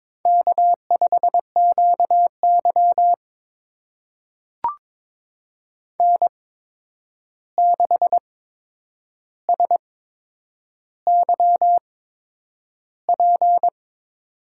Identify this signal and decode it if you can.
Morse code, heard as a keyed tone, about 22 wpm, spaced as for 9 wpm, 700 Hz. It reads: K5QY N6SYP
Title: USA Call Signs - 8x Character Spacing - 22wpm